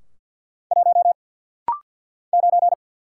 Morse code, heard as a keyed tone, about 50 wpm, 700 Hz. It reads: 1 9
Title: Single Letter-Number - Rapid-Fire 50wpm